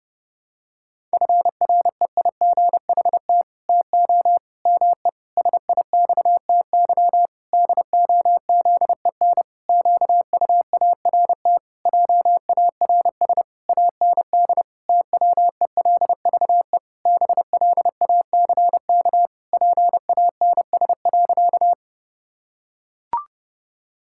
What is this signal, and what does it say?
Morse code, heard as a keyed tone, about 32 wpm, 700 Hz. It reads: FREIGHT TO ME SIXTY DOZEN QUART JARS AND TWELVE BLACK PANS.
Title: Pangram Challenge v2 - 30wpm